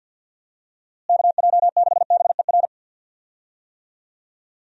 Morse code, heard as a keyed tone, about 52 wpm, 700 Hz. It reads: KJ6BER
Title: USA Call Signs - 6 Characters - Repeated 3x - 50wpm